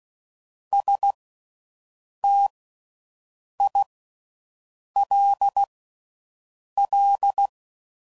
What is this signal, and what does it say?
Morse code, heard as a keyed tone, about 16 wpm, spaced as for 6 wpm, 800 Hz. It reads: STILL